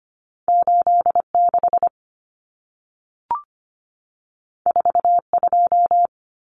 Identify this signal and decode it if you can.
Morse code, heard as a keyed tone, about 25 wpm, 700 Hz. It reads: 86 42
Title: Common Cut Numbers - 2-Digit 25wpm